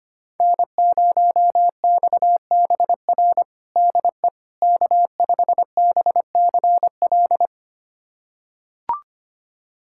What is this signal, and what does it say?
Morse code, heard as a keyed tone, about 25 wpm, 700 Hz. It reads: N0XBR DE K5BCL
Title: USA Callsign DE Callsign - 5 Characters - 25wpm